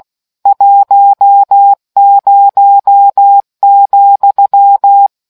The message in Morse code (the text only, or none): 10,